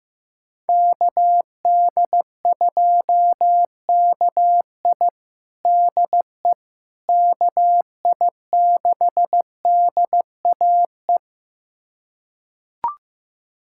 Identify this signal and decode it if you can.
Morse code, heard as a keyed tone, about 15 wpm, 700 Hz. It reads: KD2KI DE KI6DAE
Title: USA Callsign DE Callsign 15wpm